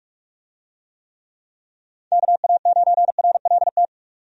Morse code, heard as a keyed tone, about 48 wpm, 700 Hz. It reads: KA9RLT
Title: USA Call Signs - 6 Characters - 45wpm